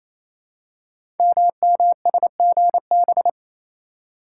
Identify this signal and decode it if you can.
Morse code, heard as a keyed tone, about 29 wpm, 700 Hz. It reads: MMSGB
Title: Code Groups - 5 Letters - 28wpm